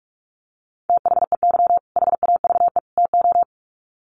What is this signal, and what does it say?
Morse code, heard as a keyed tone, about 44 wpm, 700 Hz. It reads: THEY HAVE TO
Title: Sets of 3 Words 45wpm